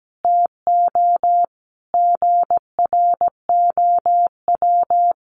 Morse code, heard as a keyed tone, about 17 wpm, 700 Hz. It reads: TO GROW
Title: Sentences from Top 1200 Words 17wpm